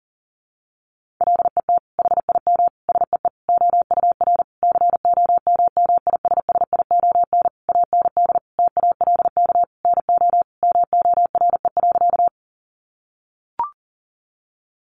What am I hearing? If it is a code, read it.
Morse code, heard as a keyed tone, about 40 wpm, 700 Hz. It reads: LET HIM SEE OUR COMMISSION AND TALK NO MORE.
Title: Sentences from Top 1400 Words 40wpm